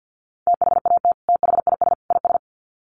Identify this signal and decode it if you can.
Morse code, heard as a keyed tone, about 48 wpm, 700 Hz. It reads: THAT THIS IS
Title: Sets of 3 Words 50wpm